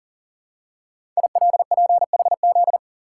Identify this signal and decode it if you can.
Morse code, heard as a keyed tone, about 42 wpm, 690 Hz. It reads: ILPHZ
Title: Code Groups - 5 Letter:Numbers - 40wpm